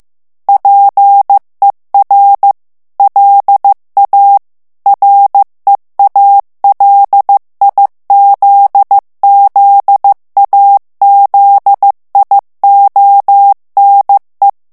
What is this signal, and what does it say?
Morse code, heard as a keyed tone, about 15 wpm, 800 Hz. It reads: PER LA REALIZZAZIONE
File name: ITALIEN texte clair 1 vitesse 14 mots minute